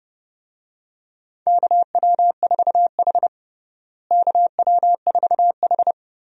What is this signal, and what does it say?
Morse code, heard as a keyed tone, about 30 wpm, 700 Hz. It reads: KW4H KW4H
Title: USA Call Signs - 4 Characters - Repeated 3x - 30wpm